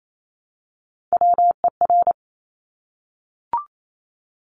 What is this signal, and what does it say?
Morse code, heard as a keyed tone, about 28 wpm, 700 Hz. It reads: WER
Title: Top Three Letter Combinations in Words 28wpm